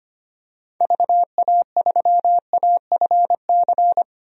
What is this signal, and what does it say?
Morse code, heard as a keyed tone, about 25 wpm, 700 Hz. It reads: VA3AFC
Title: DXCC-100 Call Signs - 6 Characters - 25wpm